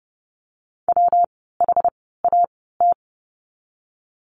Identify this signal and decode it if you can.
Morse code, heard as a keyed tone, about 30 wpm, spaced as for 17 wpm, 700 Hz. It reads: WHAT
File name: course / 30:10wpm / 115 Lesson 19f - Letter w - Words 30wpm